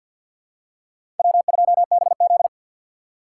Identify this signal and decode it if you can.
Morse code, heard as a keyed tone, about 47 wpm, 700 Hz. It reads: W2BZ
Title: USA Call Signs - 4 Characters - 3x Character Spacing - 50wpm